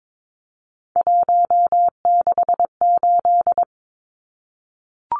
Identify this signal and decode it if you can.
Morse code, heard as a keyed tone, about 22 wpm, 700 Hz. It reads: 168
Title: Numbers - 3-Digit 22wpm